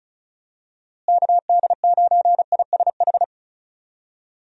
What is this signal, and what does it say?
Morse code, heard as a keyed tone, about 36 wpm, 700 Hz. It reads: KD9ISH
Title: USA Call Signs - 6 Characters - Repeated 3x - 35wpm